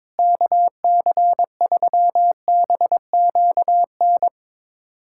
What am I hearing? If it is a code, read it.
Morse code, heard as a keyed tone, about 22 wpm, 700 Hz. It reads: KC3BQN
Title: USA Callsign DE Callsign - 6 Characters - 22wpm